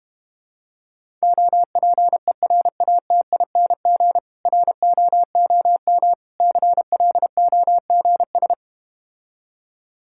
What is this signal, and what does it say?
Morse code, heard as a keyed tone, about 32 wpm, 700 Hz. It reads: OPERATING ROOM CLOGS